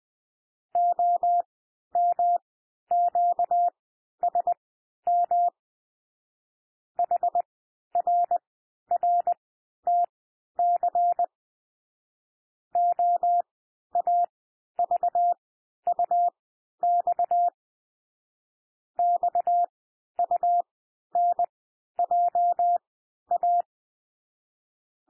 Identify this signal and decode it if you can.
Morse code, heard as a keyed tone, about 20 wpm, spaced as for 11 wpm, 700 Hz. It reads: OMQSM HRRTC OAVUX XUNJA
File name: cwmp3 / L1 SOUND LETTERS 20 wpm